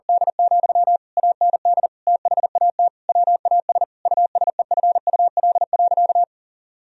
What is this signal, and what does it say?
Morse code, heard as a keyed tone, about 40 wpm, 700 Hz. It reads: D, AND THAT WAS USEFUL.
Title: Sentences from Top 1000 Words - Review Part 4 - 40wpm